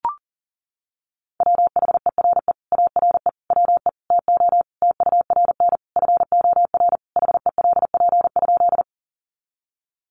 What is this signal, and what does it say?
Morse code, heard as a keyed tone, about 40 wpm, 700 Hz. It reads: WHERE ARE WE TO TURN FOR HELP?
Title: Sentences from Top 200 Words 40wpm